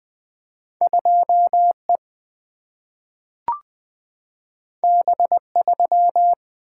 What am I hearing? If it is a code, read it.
Morse code, heard as a keyed tone, about 20 wpm, 700 Hz. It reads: 2E B3
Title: Single-Letter Single-Number 20wpm